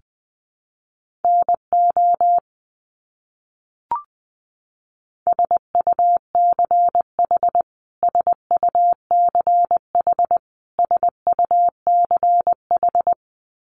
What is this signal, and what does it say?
Morse code, heard as a keyed tone, about 20 wpm, 700 Hz. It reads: NO SUCH SUCH SUCH
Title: Top 200 Words Repeated 20wpm